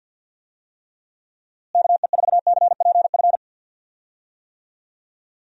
Morse code, heard as a keyed tone, about 46 wpm, 700 Hz. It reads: KE4CPF
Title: USA Call Signs - 6 Characters - Repeated 2x - 50wpm